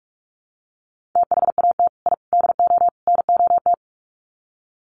Morse code, heard as a keyed tone, about 44 wpm, 700 Hz. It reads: THAT I DO NOT
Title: Sets of 4 Words 45wpm